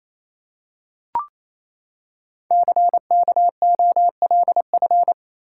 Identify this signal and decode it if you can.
Morse code, heard as a keyed tone, about 29 wpm, 700 Hz. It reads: CKOLF
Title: Code Groups - 5 Letters - 28wpm